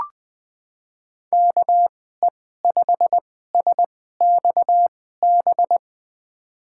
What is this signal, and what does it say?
Morse code, heard as a keyed tone, about 20 wpm, spaced as for 14 wpm, 700 Hz. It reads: KE5SXB
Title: USA Call Signs - 6 Characters - 2x Character Spacing - 20wpm